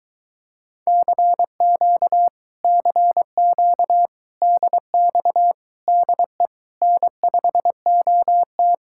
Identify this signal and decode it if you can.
Morse code, heard as a keyed tone, about 22 wpm, 700 Hz. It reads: CQ CQ DX DE N5OT